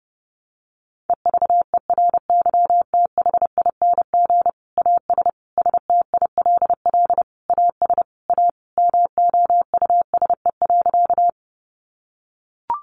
Morse code, heard as a keyed tone, about 30 wpm, 700 Hz. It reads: EVERYTHING AS STILL AS A MOUSE.
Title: Sentences from Top 700 Words 30wpm